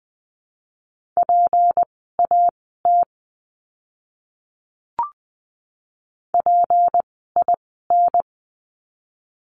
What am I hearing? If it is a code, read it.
Morse code, heard as a keyed tone, about 20 wpm, spaced as for 14 wpm, 700 Hz. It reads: PAT PIN
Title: 153 Lesson 25f - Letter p - Words 20wpm